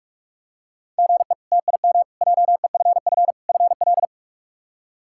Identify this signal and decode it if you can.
Morse code, heard as a keyed tone, about 43 wpm, 700 Hz. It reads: GE TIM JEFF FL
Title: K1USN SST QSO Practice - 45wpm